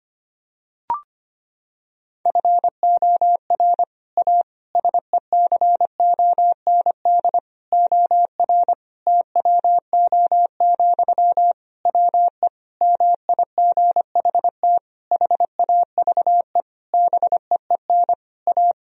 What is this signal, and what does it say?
Morse code, heard as a keyed tone, about 25 wpm, 700 Hz. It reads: FOR A SECOND OR TWO, WE MIGHT HAVE BEEN A
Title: Sentences from Top 1000 Words - Review Part 1 - 25wpm